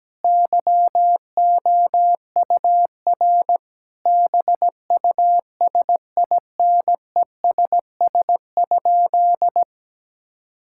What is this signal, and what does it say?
Morse code, heard as a keyed tone, about 17 wpm, 700 Hz. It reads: YOUR BUSINESS?
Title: Sentences from Top 1000 Words - Review Part 5 - 17wpm